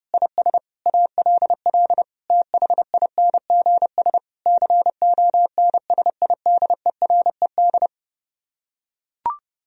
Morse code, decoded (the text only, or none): IS ALL THINGS CONSIDERED